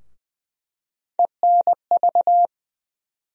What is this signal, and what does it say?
Morse code, heard as a keyed tone, about 20 wpm, 700 Hz. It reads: ENV
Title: USA Call Sign Postfixes - 3 Letters 20wpm